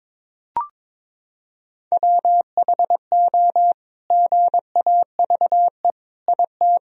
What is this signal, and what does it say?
Morse code, heard as a keyed tone, about 22 wpm, 700 Hz. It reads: WHO GAVE IT
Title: Sentences from Top 500 Words and Abbreviations 22wpm